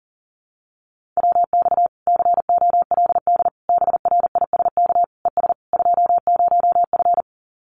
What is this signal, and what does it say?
Morse code, heard as a keyed tone, about 37 wpm, 700 Hz. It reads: WX COLD BRISK ES 20F
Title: Common QSO Phrases Encore 40wpm